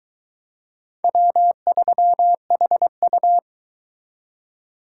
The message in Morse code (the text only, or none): W3HU